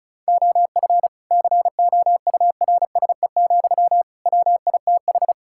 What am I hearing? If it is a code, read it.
Morse code, heard as a keyed tone, about 36 wpm, 700 Hz. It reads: OF COURSE, WITH
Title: Sentences from Top 1600 Words 35wpm